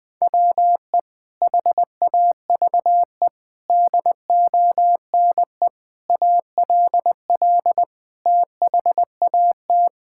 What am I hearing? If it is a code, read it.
Morse code, heard as a keyed tone, about 20 wpm, 700 Hz. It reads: WE HAVE DONE ALL THAT
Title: Sentences from Top 1000 Words - Review Part 5 - 20wpm